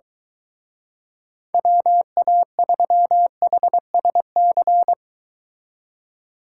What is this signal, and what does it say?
Morse code, heard as a keyed tone, about 24 wpm, 700 Hz. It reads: WA3HSC